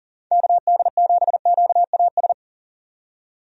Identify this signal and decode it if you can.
Morse code, heard as a keyed tone, about 40 wpm, 700 Hz. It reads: KD7QAS
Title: USA Call Signs - 6 Characters - 40wpm